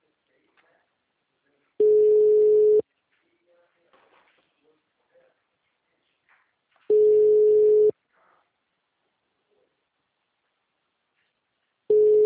Ringback tone (Europe, 425 Hz).
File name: out- -2002- - - .33806